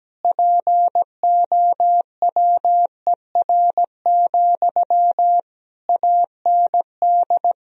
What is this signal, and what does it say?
Morse code, heard as a keyed tone, about 17 wpm, 700 Hz. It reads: POWER, AND
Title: Sentences from Top 1000 Words - Review Part 2 - 17wpm